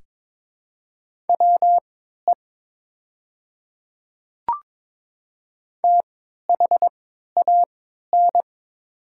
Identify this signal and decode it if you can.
Morse code, heard as a keyed tone, about 22 wpm, spaced as for 12 wpm, 690 Hz. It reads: WE THAN
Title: 3x Character Spacing - Top 100 Words - 22wpm